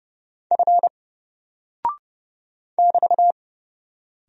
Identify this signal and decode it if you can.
Morse code, heard as a keyed tone, about 30 wpm, 700 Hz. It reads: F <BT>
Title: Single Character (Letter:Number:Punctuation:Prosign) Alt 30wpm